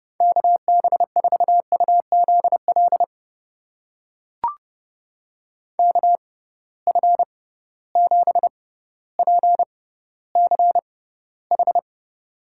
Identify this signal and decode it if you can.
Morse code, heard as a keyed tone, about 30 wpm, spaced as for 15 wpm, 700 Hz. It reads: KB4UZL KF7PCH
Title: USA Call Signs - 6 Characters - 6x Character Spacing - 30wpm